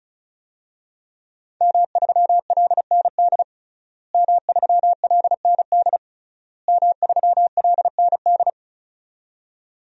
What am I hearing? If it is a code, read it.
Morse code, heard as a keyed tone, about 35 wpm, 700 Hz. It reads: M3LND M3LND M3LND
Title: DXCC-100 Call Signs - 5 Characters - Repeated 3x - 35wpm